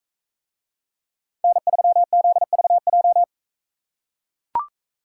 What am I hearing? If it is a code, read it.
Morse code, heard as a keyed tone, about 41 wpm, 700 Hz. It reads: N3ZVJ